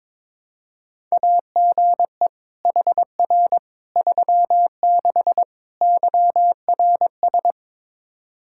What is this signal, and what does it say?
Morse code, heard as a keyed tone, about 22 wpm, 700 Hz. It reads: AGE HR 36 YRS
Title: Practice QSO - Exchange 3 - More Info - 22wpm